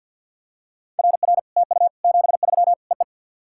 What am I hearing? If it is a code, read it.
Morse code, heard as a keyed tone, about 55 wpm, 700 Hz. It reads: RR TU 73 EE
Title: SOTA QSO Practice 50wpm